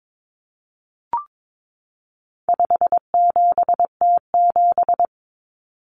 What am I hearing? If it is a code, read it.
Morse code, heard as a keyed tone, about 22 wpm, 700 Hz. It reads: 57T7
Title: Common Cut Numbers - 4-Digit 22wpm